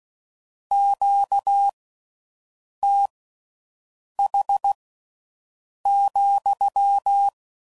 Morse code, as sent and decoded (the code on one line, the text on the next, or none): --.- - .... --..--
QTH,